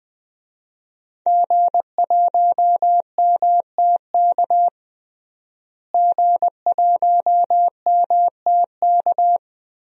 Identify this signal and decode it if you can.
Morse code, heard as a keyed tone, about 20 wpm, 700 Hz. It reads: G1MTK G1MTK
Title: DXCC-100 Call Signs - 5 Characters - Repeated 2x - 20wpm